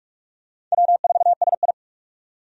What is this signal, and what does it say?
Morse code, heard as a keyed tone, about 46 wpm, 700 Hz. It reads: W4SI
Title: USA Call Signs - 4 Characters - 45wpm